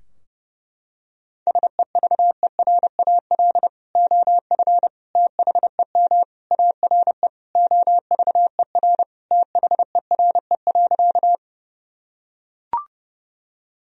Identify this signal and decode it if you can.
Morse code, heard as a keyed tone, about 30 wpm, 700 Hz. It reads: SEVERAL OF THEM ARE OVER THERE.
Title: Sentences from Top 1000 Words - Review Part 2 - 30wpm